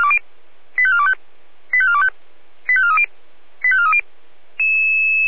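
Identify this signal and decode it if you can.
Five-tone selective call (ZVEI), heard as quick runs of five tones, tones of about 70 ms. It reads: 85425 85425 85430 85430
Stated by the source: ZVEI five-tone call 85410 received on an SDR.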